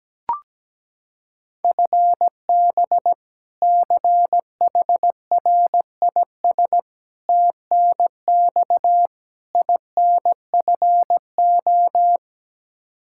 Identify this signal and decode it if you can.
Morse code, heard as a keyed tone, about 17 wpm, 700 Hz. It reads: FB CHRIS TNX INFO
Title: Practice QSO - Exchange 3 - More Info - 17wpm